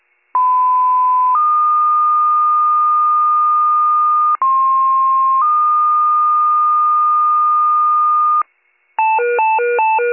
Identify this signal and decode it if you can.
Two-tone page, heard as a one-second tone, then a three-second tone, about 1010 then 1220 Hz, about 1060 then 1220 Hz. It is heard from 0.3 to 8.4 s.